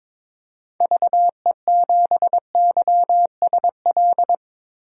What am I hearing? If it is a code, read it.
Morse code, heard as a keyed tone, about 22 wpm, 700 Hz. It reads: VE7YSL
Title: DXCC-100 Call Signs - 6 Characters - Repeated 2x - 22wpm